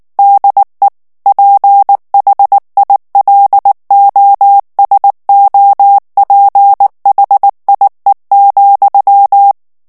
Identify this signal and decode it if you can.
Morse code, heard as a keyed tone, about 19 wpm, 800 Hz. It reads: DE PHILOSOPHIE,
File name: CW / FICHIERS-MP3 / FRANCAIS TEXTE CLAIR 2 VITESSE 18 MOTS MINUTES